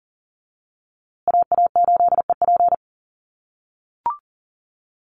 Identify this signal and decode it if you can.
Morse code, heard as a keyed tone, about 40 wpm, 700 Hz. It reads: AA8EP
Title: USA Call Signs Repeated 3x 40wpm